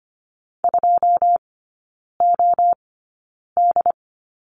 Morse code, heard as a keyed tone, about 25 wpm, spaced as for 9 wpm, 700 Hz. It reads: 2OD
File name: Lesson 14e - Number 2 - Three Characters 25wpm